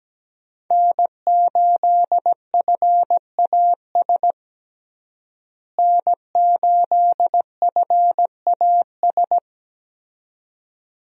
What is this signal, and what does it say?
Morse code, heard as a keyed tone, about 17 wpm, 700 Hz. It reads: N8FAS N8FAS
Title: USA Call Signs - 5 Characters - Repeated 2x - 17wpm